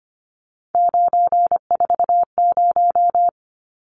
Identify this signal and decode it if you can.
Morse code, heard as a keyed tone, about 25 wpm, 700 Hz. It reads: 940
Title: Numbers - 3-Digit 25wpm